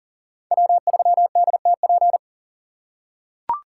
Morse code, heard as a keyed tone, about 40 wpm, 700 Hz. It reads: W3DTP
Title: USA Call Signs - 5 Characters - 3x Character Spacing - 40wpm